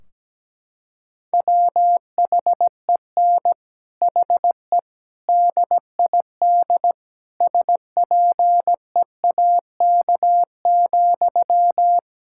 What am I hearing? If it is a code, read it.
Morse code, heard as a keyed tone, about 17 wpm, 700 Hz. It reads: WHEN HE DID SPEAK,
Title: Sentences from Top 1600 Words 17wpm